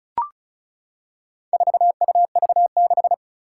Morse code, heard as a keyed tone, about 36 wpm, 700 Hz. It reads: 4UV6
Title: Four Challenging Letter-Number Combinations 35wpm